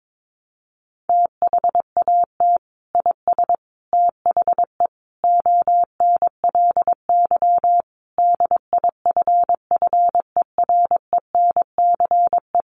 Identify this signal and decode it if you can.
Morse code, heard as a keyed tone, about 22 wpm, 700 Hz. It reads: THAT IS THE ONLY DIFFERENCE
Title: Sentences from Top 800 Words 22wpm